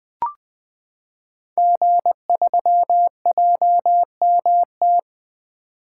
Morse code, heard as a keyed tone, about 20 wpm, 700 Hz. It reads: G3JMT
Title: Code Groups - 5 Letter:Numbers - 20wpm